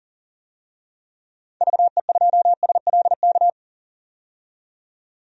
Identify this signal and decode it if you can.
Morse code, heard as a keyed tone, about 40 wpm, 700 Hz. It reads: VE2SLK
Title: DXCC-100 Call Signs - 6 Characters - 40wpm